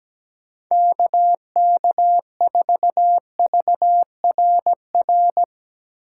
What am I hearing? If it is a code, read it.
Morse code, heard as a keyed tone, about 17 wpm, 700 Hz. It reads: KK4VRR
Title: USA Call Signs - 6 Characters - 3x Character Spacing - 17wpm